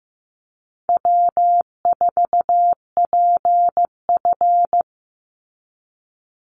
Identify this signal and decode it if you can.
Morse code, heard as a keyed tone, about 15 wpm, 700 Hz. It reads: W4PF
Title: Easy Call Signs - 15wpm